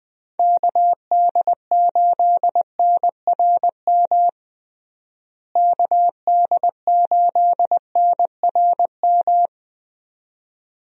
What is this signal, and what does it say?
Morse code, heard as a keyed tone, about 20 wpm, 700 Hz. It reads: KD8NRM KD8NRM
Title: USA Call Signs - 6 Characters - Repeated 2x - 20wpm